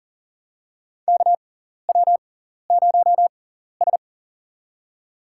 Morse code, heard as a keyed tone, about 40 wpm, spaced as for 14 wpm, 700 Hz. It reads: KW0S
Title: USA Call Signs - 6x Character Spacing - 40wpm